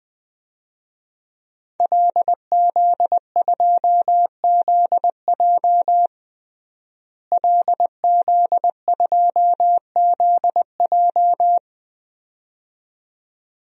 Morse code, heard as a keyed tone, about 20 wpm, 700 Hz. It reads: LZ2ZJ LZ2ZJ
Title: DXCC-100 Call Signs - 5 Characters - Repeated 2x - 20wpm